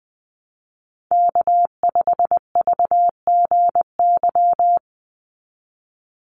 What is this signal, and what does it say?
Morse code, heard as a keyed tone, about 20 wpm, 700 Hz. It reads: K5VGY
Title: USA Call Signs 20wpm